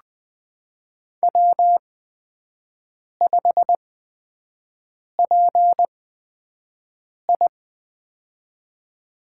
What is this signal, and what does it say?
Morse code, heard as a keyed tone, about 20 wpm, spaced as for 5 wpm, 700 Hz. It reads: W5PI
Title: USA Call Signs - 8x Character Spacing - 20wpm